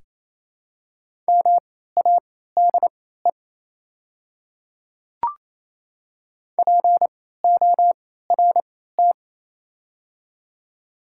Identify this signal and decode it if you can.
Morse code, heard as a keyed tone, about 28 wpm, spaced as for 16 wpm, 700 Hz. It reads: MADE PORT
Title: 3x Character Spacing - Top 500 Words - 28wpm